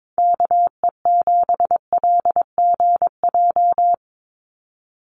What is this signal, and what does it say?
Morse code, heard as a keyed tone, about 22 wpm, 700 Hz. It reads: KE7LGJ
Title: USA Call Signs 22wpm